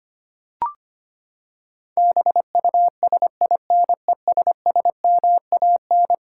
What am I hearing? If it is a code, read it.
Morse code, heard as a keyed tone, about 25 wpm, 700 Hz. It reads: BUSINESSMAN
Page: morse-code-ninja-practice-25wpm